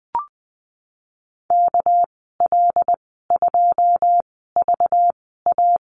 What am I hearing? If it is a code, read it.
Morse code, heard as a keyed tone, about 20 wpm, spaced as for 15 wpm, 700 Hz. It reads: KL2VA
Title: 207 Lesson 33g - Letter j - Call Signs 20wpm